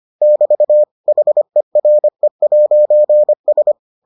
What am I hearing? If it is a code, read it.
Morse code, heard as a keyed tone, about 25 wpm, 600 Hz. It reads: <BT> HERE'S